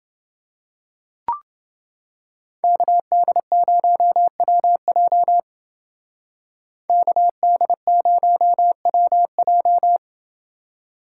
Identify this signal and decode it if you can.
Morse code, heard as a keyed tone, about 29 wpm, 700 Hz. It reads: KD0WJ KD0WJ